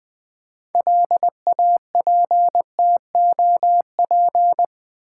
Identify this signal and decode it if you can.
Morse code, heard as a keyed tone, about 20 wpm, 700 Hz. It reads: LAPTOP
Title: 6 Letter Words - First 3 Letters Given - 20wpm